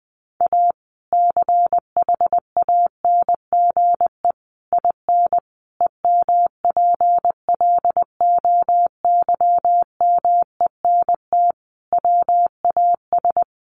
Morse code, heard as a keyed tone, about 20 wpm, 700 Hz. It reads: A CHANGE IN EMPLOYMENT WAS
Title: Sentences from Top 1500 Words 20wpm